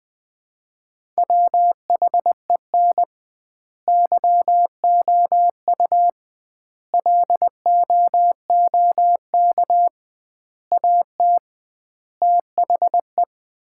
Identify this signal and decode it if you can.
Morse code, heard as a keyed tone, about 21 wpm, 700 Hz. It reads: WHEN YOU LOOK AT THE
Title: 2x Word-Spacing - Sets of 5 Words - 20wpm